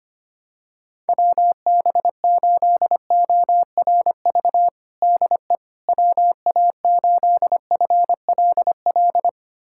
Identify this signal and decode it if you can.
Morse code, heard as a keyed tone, about 25 wpm, 700 Hz. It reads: WB8ORV DE WA8FLL
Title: USA Callsign DE Callsign - 6 Characters - 25wpm